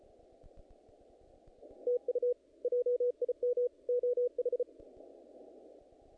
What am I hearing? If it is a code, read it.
Morse code, heard as a keyed tone, about 32 wpm, 490 Hz. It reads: TU JIM OH
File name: contests / naqp-cw-k5tr-op-k5pi- -2000